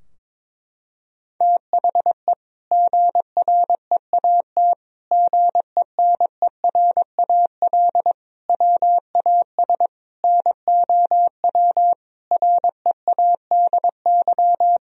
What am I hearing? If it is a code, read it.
Morse code, heard as a keyed tone, about 22 wpm, 700 Hz. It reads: THE GREAT GENERAL WAS NOW READY
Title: Sentences from Top 1000 Words - Review Part 4 - 22wpm